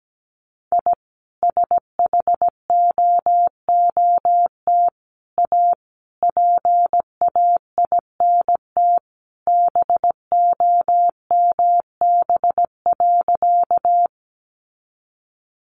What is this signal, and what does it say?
Morse code, heard as a keyed tone, about 17 wpm, 700 Hz. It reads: I SHOOT A PAINT BOMB.
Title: A Sound Of Thunder 17wpm